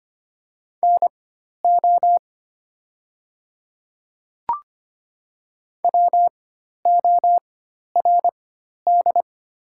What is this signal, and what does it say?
Morse code, heard as a keyed tone, about 25 wpm, spaced as for 12 wpm, 690 Hz. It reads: NO WORD
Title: 4x Character Spacing - Top 100 Words - 25wpm